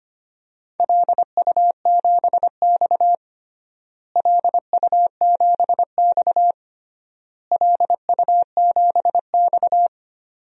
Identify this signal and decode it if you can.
Morse code, heard as a keyed tone, about 25 wpm, 700 Hz. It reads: LU7X LU7X LU7X
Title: DXCC-100 Call Signs - 4 Characters - Repeated 3x - 25wpm